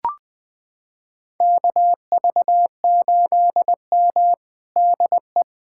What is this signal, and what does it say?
Morse code, heard as a keyed tone, about 20 wpm, 700 Hz. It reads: KV8M DE
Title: USA Callsign DE Callsign - 4 Characters - 20wpm